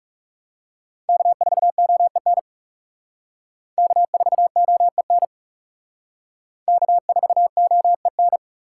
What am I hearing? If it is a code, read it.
Morse code, heard as a keyed tone, about 37 wpm, 700 Hz. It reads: K4OEN K4OEN K4OEN
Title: Speed Racing - USA 1x3 Call Signs 30wpm